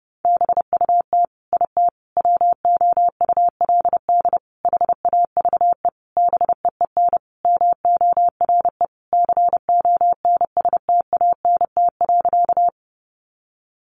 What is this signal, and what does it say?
Morse code, heard as a keyed tone, about 30 wpm, 700 Hz. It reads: BUT IT WOULD HAVE BEEN MORE CONSTANT.
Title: Sentences from Top 1500 Words 30wpm